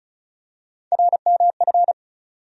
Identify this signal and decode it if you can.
Morse code, heard as a keyed tone, about 36 wpm, 700 Hz. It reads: RMF
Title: USA Call Sign Postfixes - 3 Letters 35wpm